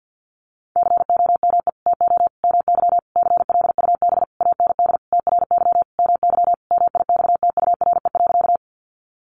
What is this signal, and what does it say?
Morse code, heard as a keyed tone, about 48 wpm, 700 Hz. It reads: COME TO MY CLUB AND TRY MY MIXTURE.
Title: Sentences from Top 1500 Words 50wpm